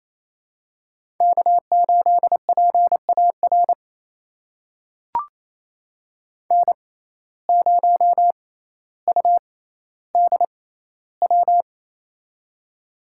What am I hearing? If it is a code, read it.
Morse code, heard as a keyed tone, about 28 wpm, spaced as for 14 wpm, 700 Hz. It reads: K8PAR N0UDW
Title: USA Call Signs - 5 Characters - 6x Character Spacing - 28wpm